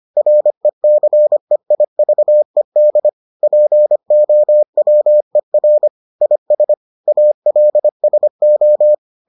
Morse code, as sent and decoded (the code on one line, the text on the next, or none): .-. -.-. .. ...- . -.. / .--. --- .-- . .-. / .. ... / .- .-.. ... ---
RECEIVED POWER IS ALSO